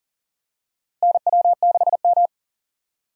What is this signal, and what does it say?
Morse code, heard as a keyed tone, about 40 wpm, 700 Hz. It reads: NW6M
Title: USA Call Signs - 4 Characters - Repeated 3x - 40wpm